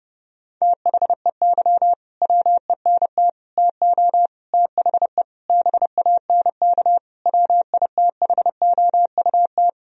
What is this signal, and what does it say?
Morse code, heard as a keyed tone, about 30 wpm, 700 Hz. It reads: THEY WENT TO THE BANK WITHOUT
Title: Sentences from Top 1000 Words - Review Part 4 - 30wpm